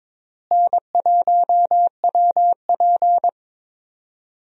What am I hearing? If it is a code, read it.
Morse code, heard as a keyed tone, about 22 wpm, 700 Hz. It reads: N1WP
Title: USA Call Signs - 4 Characters - Repeated 3x - 22wpm